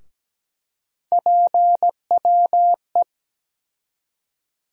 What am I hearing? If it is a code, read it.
Morse code, heard as a keyed tone, about 17 wpm, 700 Hz. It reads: PWE
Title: USA Call Sign Postfixes - 3 Letters 17wpm